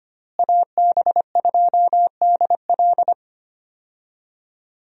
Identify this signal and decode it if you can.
Morse code, heard as a keyed tone, about 25 wpm, 700 Hz. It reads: AB2DL
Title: USA Call Signs - 5 Characters - Repeated 2x - 25wpm